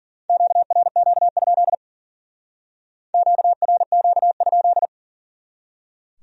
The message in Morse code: QRQ? QRQ?

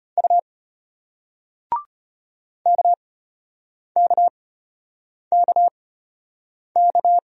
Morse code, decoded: U KKKK

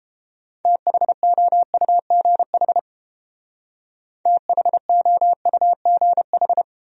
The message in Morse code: THOUGH THOUGH

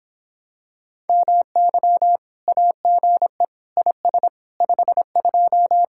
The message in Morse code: MY AGE IS 52

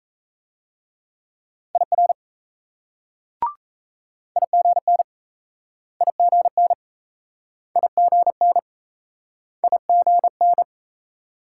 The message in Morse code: IR IGN IGN IGN IGN

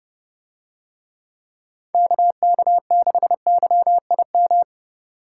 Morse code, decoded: KK6YIM

